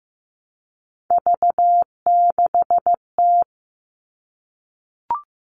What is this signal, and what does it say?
Morse code, heard as a keyed tone, about 15 wpm, 700 Hz. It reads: V6T
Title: All Cut Numbers - 3-Digit 15wpm